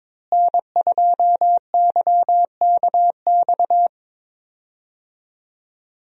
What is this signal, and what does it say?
Morse code, heard as a keyed tone, about 22 wpm, 700 Hz. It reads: N2YKX
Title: USA Call Signs - 5 Characters - Repeated 2x - 22wpm